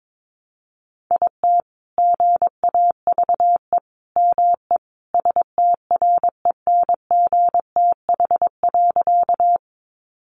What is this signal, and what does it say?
Morse code, heard as a keyed tone, about 22 wpm, 700 Hz. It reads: IT GAVE ME STRENGTH.
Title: Sentences from Top 1300 Words 22wpm